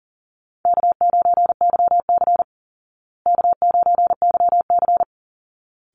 Morse code, heard as a keyed tone, about 40 wpm, 700 Hz. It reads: K9YC K9YC
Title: Call Signs Repeated 3x - 40wpm